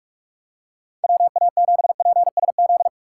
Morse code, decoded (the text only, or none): WA7PSZ